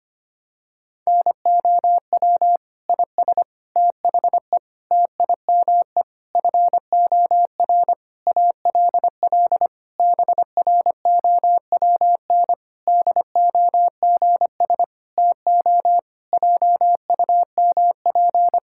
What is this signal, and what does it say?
Morse code, heard as a keyed tone, about 25 wpm, 700 Hz. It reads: NOW IS THE TIME FOR ALL BROWN DOGS TO JUMP
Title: Pangram Challenge v2 - 25wpm